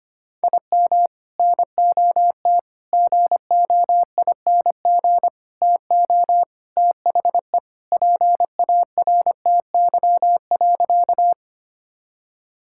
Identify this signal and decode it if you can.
Morse code, heard as a keyed tone, about 25 wpm, 700 Hz. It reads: IM NOT GOING TO THE PARTY.
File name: practice / 25wpm / Sentences with 3-Word Phrases - 25wpm